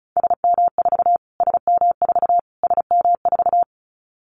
Morse code, heard as a keyed tone, about 34 wpm, 700 Hz. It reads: SM4 SM4 SM4
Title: Callsign Prefixes for Top 25 DXCC - Repeated 3x - 35wpm